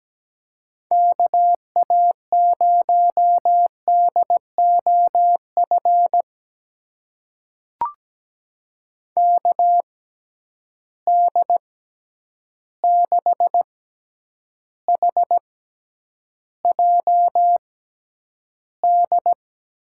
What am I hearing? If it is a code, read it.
Morse code, heard as a keyed tone, about 17 wpm, spaced as for 9 wpm, 700 Hz. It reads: KA0DOF KD6HJD
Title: USA Call Signs - 6 Characters - 6x Character Spacing - 17wpm